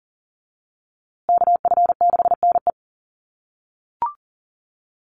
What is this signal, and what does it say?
Morse code, heard as a keyed tone, about 40 wpm, 700 Hz. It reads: KF6NE
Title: USA Call Signs Repeated 3x 40wpm